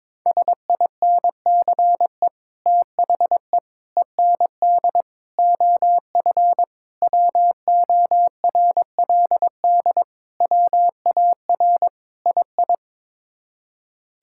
Morse code, heard as a keyed tone, about 22 wpm, 700 Hz. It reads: SINCE THE END OF WORLD WAR II
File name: practice / 22wpm / Sets of 7 Words - 22wpm